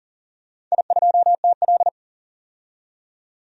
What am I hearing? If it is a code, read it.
Morse code, heard as a keyed tone, about 44 wpm, 690 Hz. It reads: I2TL
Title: DXCC-100 Call Signs - 4 Characters - 40wpm